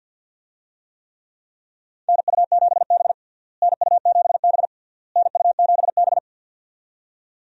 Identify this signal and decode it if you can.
Morse code, heard as a keyed tone, about 53 wpm, 700 Hz. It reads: NU7B NU7B NU7B
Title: USA Call Signs - 4 Characters - Repeated 3x - 50wpm